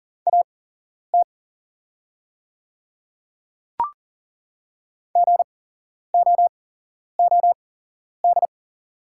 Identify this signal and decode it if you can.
Morse code, heard as a keyed tone, about 40 wpm, spaced as for 11 wpm, 700 Hz. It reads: AT GOOD
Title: 8x Character Spacing - Top 300 Words - 40wpm